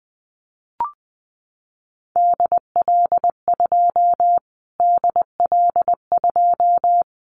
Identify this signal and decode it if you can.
Morse code, heard as a keyed tone, about 20 wpm, 700 Hz. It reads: DL2 DL2
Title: Callsign Prefixes for Top 25 DXCC - Repeated 3x - 20wpm